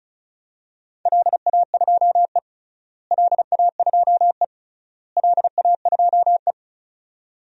Morse code, heard as a keyed tone, about 36 wpm, 700 Hz. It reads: LA2E LA2E LA2E
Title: DXCC-100 Call Signs - 4 Characters - Repeated 3x - 35wpm